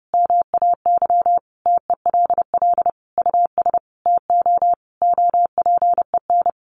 Morse code, heard as a keyed tone, about 29 wpm, 700 Hz. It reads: MAY TELL US TO OPEN
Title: Sentences from Top 300 Words 30wpm